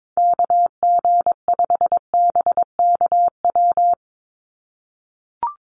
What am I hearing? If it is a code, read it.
Morse code, heard as a keyed tone, about 22 wpm, 700 Hz. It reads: KG5BKW
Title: USA Call Signs 22wpm